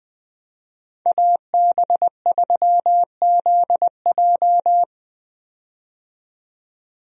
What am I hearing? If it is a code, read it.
Morse code, heard as a keyed tone, about 20 wpm, 700 Hz. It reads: AB3ZJ
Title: USA Call Signs - 5 Characters - 20wpm